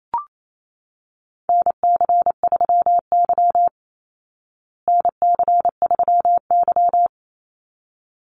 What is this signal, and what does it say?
Morse code, heard as a keyed tone, about 28 wpm, 700 Hz. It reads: NC3Y NC3Y
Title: Call Signs Repeated 3x - 28wpm